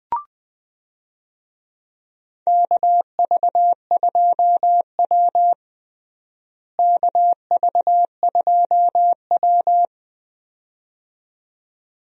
Morse code, heard as a keyed tone, about 20 wpm, 700 Hz. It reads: KV2W KV2W
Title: USA Call Signs - 4 Characters - Repeated 2x - 20wpm